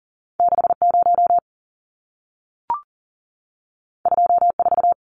Morse code, heard as a keyed tone, about 40 wpm, 700 Hz. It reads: 60 24
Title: Numbers - 2-Digit 40wpm